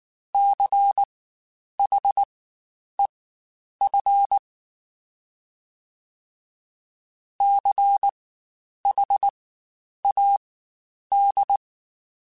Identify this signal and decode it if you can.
Morse code, heard as a keyed tone, about 19 wpm, spaced as for 9 wpm, 800 Hz. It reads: CHEF CHAD
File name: Spaced Alphas / Spaced Alphas mp3s / Cs sp @18